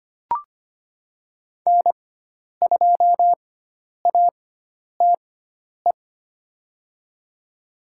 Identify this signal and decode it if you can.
Morse code, heard as a keyed tone, about 25 wpm, spaced as for 10 wpm, 700 Hz. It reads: N2ATE
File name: Lesson 16g - Letter c - Call Signs 25wpm